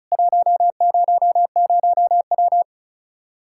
Morse code, heard as a keyed tone, about 34 wpm, 700 Hz. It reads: 100W
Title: K1USN SST QSO Practice 35wpm